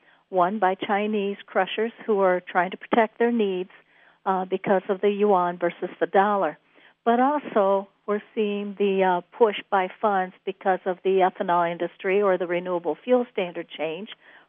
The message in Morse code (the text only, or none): none